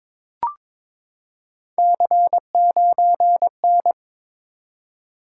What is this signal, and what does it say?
Morse code, heard as a keyed tone, about 22 wpm, 700 Hz. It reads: C9N
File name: practice / 22wpm / Three Letter-Number Combinations - 22wpm